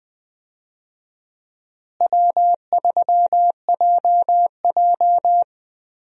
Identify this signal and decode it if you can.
Morse code, heard as a keyed tone, about 20 wpm, 700 Hz. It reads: W3JJ
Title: USA Call Signs - 4 Characters - Repeated 3x - 20wpm